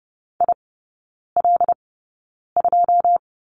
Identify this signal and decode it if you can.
Morse code, heard as a keyed tone, about 30 wpm, spaced as for 9 wpm, 700 Hz. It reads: IL2
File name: Lesson 15e - Number 5 - Three Characters 30wpm